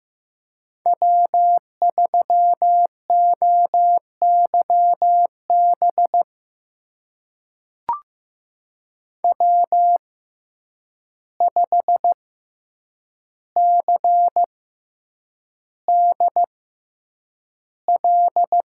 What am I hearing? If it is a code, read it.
Morse code, heard as a keyed tone, about 15 wpm, spaced as for 8 wpm, 700 Hz. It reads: W3OYB W5CDL
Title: USA Call Signs - 5 Characters - 6x Character Spacing - 15wpm